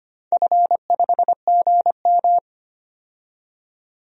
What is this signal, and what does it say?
Morse code, heard as a keyed tone, about 25 wpm, 700 Hz. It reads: F5GM
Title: DXCC-100 Call Signs - 4 Characters - 25wpm